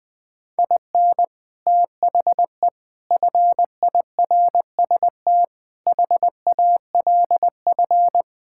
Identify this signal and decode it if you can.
Morse code, heard as a keyed tone, about 20 wpm, 700 Hz. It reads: IN THE FIRST HALF
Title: Sentences with 5-Word Phrases - 20wpm